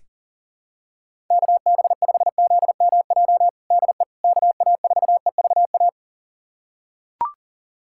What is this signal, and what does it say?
Morse code, heard as a keyed tone, about 40 wpm, 700 Hz. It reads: KB5ZMJ DE KA4EVA
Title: USA Callsign DE Callsign - 6 Characters - 40wpm